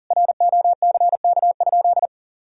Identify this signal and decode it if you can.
Morse code, heard as a keyed tone, about 40 wpm, 700 Hz. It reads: ROCK?
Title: Sentences from Top 1000 Words - Review Part 2 - 40wpm